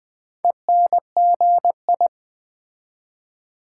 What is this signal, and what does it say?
Morse code, heard as a keyed tone, about 20 wpm, 700 Hz. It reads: ENGI
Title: Build A Word - 2 Letters At A Time - 6-Letter Words - 20wpm